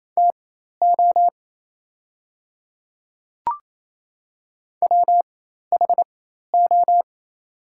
Morse code, heard as a keyed tone, about 28 wpm, spaced as for 13 wpm, 700 Hz. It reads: TO WHO
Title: 4x Character Spacing - Top 100 Words - 28wpm